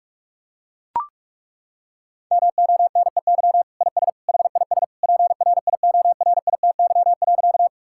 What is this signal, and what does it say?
Morse code, heard as a keyed tone, about 41 wpm, 700 Hz. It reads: MONEY IS HIS PRIORITY.
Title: Sentences from Top 1600 Words 45wpm